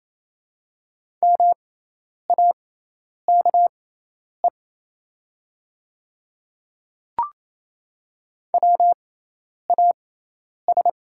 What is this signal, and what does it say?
Morse code, heard as a keyed tone, about 28 wpm, spaced as for 10 wpm, 690 Hz. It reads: MAKE WAS
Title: 6x Character Spacing - Top 100 Words - 28wpm